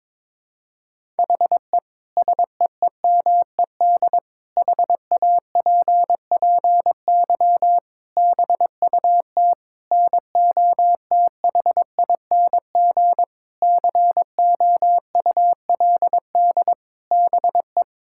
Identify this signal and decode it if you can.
Morse code, heard as a keyed tone, about 21 wpm, 700 Hz. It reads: HE SEEMED HAPPY BUT NOTHING COULD BE
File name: practice / 22wpm / Sentences with 8-Word Phrases - 22wpm